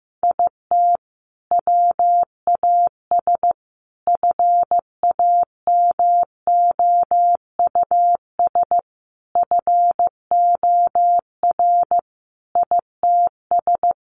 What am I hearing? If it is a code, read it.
Morse code, heard as a keyed tone, about 15 wpm, 700 Hz. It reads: IT WAS FAMOUS FOR ITS
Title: Sentences from Top 1500 Words 15wpm